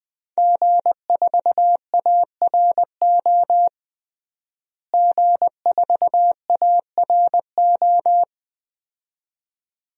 Morse code, heard as a keyed tone, about 20 wpm, 700 Hz. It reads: G4ARO G4ARO
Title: DXCC-100 Call Signs - 5 Characters - Repeated 2x - 20wpm